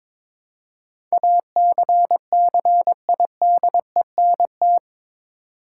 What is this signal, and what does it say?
Morse code, heard as a keyed tone, about 22 wpm, 700 Hz. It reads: ACCIDENT
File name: practice / 22wpm / Eight Letter Words - 22wpm